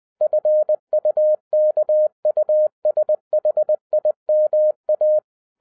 Morse code, heard as a keyed tone, about 20 wpm, 610 Hz. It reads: FUKUSHIMA
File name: 2011-10-mp3 / n03-TARO JR0QWW 20wpm